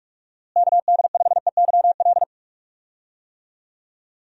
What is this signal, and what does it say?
Morse code, heard as a keyed tone, about 46 wpm, 700 Hz. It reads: KD5EYL
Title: USA Call Signs - 6 Characters - Repeated 3x - 45wpm